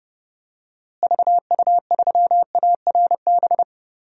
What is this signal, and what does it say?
Morse code, heard as a keyed tone, about 30 wpm, 700 Hz. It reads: VU3ARB
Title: DXCC-100 Call Signs - 6 Characters - Repeated 2x - 30wpm